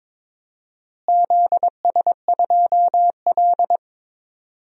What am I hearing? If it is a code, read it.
Morse code, heard as a keyed tone, about 22 wpm, 700 Hz. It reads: ZS2L
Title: DXCC-100 Call Signs - 4 Characters - Repeated 3x - 22wpm